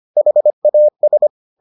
Morse code, heard as a keyed tone, about 25 wpm, 600 Hz. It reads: HAS